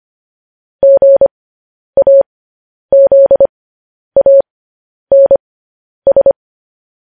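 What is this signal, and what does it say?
Morse code, heard as a keyed tone, about 25 wpm, spaced as for 10 wpm, 550 Hz. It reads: GAZANS
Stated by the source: Morse code news headlines practice at 10 wpm.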